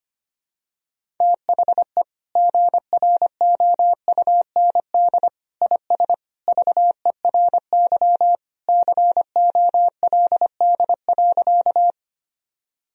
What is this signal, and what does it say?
Morse code, heard as a keyed tone, about 25 wpm, 700 Hz. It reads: THE GROUND IS VERY COLD.
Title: Sentences from Top 1000 Words - Review Part 2 - 25wpm